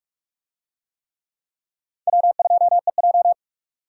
Morse code, heard as a keyed tone, about 44 wpm, 700 Hz. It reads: W2EJ